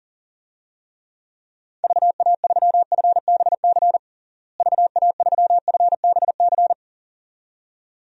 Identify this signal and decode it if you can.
Morse code, heard as a keyed tone, about 37 wpm, 700 Hz. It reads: VA3FBC VA3FBC
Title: DXCC-100 Call Signs - 6 Characters - Repeated 2x - 40wpm